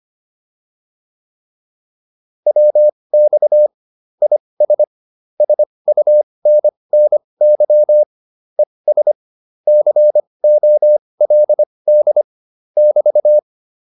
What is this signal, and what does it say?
Morse code, heard as a keyed tone, about 25 wpm, spaced as for 20 wpm, 600 Hz. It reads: WX IS SUNNY ES COLD <BT>